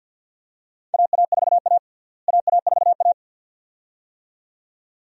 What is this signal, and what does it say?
Morse code, heard as a keyed tone, about 49 wpm, 700 Hz. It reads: AA4A AA4A
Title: USA Call Signs - 4 Characters - Repeated 2x - 50wpm